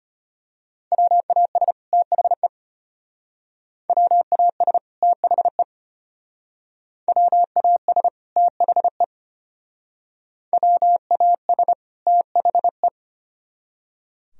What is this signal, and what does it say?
Morse code, heard as a keyed tone, about 31 wpm, 700 Hz. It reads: WAS THE WAS THE WAS THE WAS THE
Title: Speed-Racing - Sets of 2 Words 25wpm